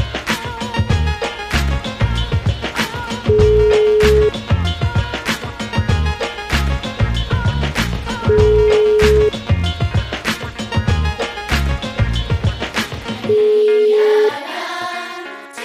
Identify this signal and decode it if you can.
Ringback tone (Europe, 425 Hz), with noise.